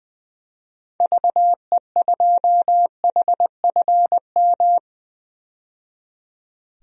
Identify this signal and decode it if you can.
Morse code, heard as a keyed tone, about 20 wpm, 700 Hz. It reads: VE2HFM
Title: DXCC-100 Call Signs - 6 Characters - 20wpm